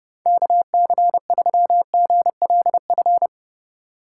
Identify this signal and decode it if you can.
Morse code, heard as a keyed tone, about 29 wpm, 700 Hz. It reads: KC3GLF